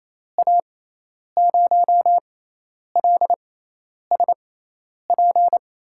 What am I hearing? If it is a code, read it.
Morse code, heard as a keyed tone, about 28 wpm, spaced as for 10 wpm, 700 Hz. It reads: A0LSP
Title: USA Call Signs - 6x Character Spacing - 28wpm